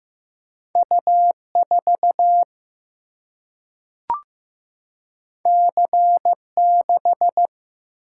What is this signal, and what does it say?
Morse code, heard as a keyed tone, about 15 wpm, 700 Hz. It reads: U4 C6
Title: Single-Letter Single-Number 15wpm